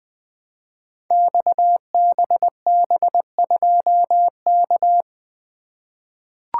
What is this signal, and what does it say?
Morse code, heard as a keyed tone, about 20 wpm, 700 Hz. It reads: XBB2K
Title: Code Groups - 5 Letter:Numbers - 20wpm